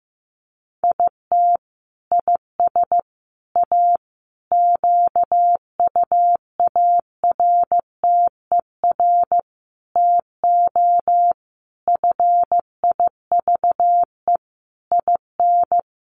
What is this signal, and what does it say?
Morse code, heard as a keyed tone, about 15 wpm, 700 Hz. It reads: IT IS A QUARTER TO FIVE IN
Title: Sentences from Top 1300 Words 15wpm